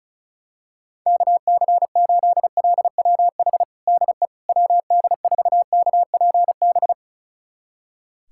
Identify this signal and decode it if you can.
Morse code, heard as a keyed tone, about 36 wpm, 700 Hz. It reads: KC8LWH DE WD4KPB
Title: USA Callsign DE Callsign 35wpm